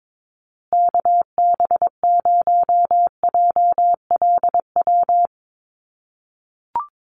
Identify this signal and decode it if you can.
Morse code, heard as a keyed tone, about 22 wpm, 700 Hz. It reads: KB0JLW
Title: USA Call Signs Repeated 2x 22wpm